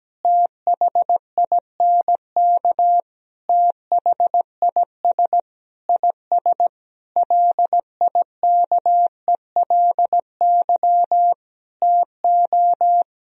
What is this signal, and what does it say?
Morse code, heard as a keyed tone, about 17 wpm, 700 Hz. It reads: THINK THIS IS LIKELY TO
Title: Sentences from Top 1000 Words - Review Part 4 - 17wpm